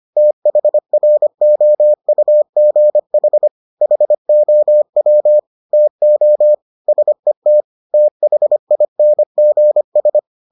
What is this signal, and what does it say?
Morse code, heard as a keyed tone, about 25 wpm, 600 Hz. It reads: THROUGH HOW TO SET THINGS